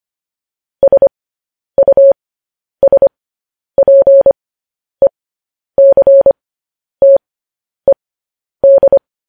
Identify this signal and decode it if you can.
Morse code, heard as a keyed tone, about 25 wpm, spaced as for 10 wpm, 550 Hz. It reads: SUSPECTED
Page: news headlines in morse code at 10 wpm